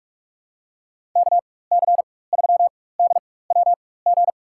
Morse code, heard as a keyed tone, about 46 wpm, spaced as for 21 wpm, 700 Hz. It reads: KC3DWG
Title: USA Call Signs - 6 Characters - 4x Character Spacing - 45wpm